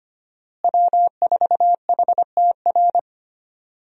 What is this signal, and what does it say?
Morse code, heard as a keyed tone, about 25 wpm, 700 Hz. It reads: W4HTR
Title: USA Call Signs - 5 Characters - Repeated 3x - 25wpm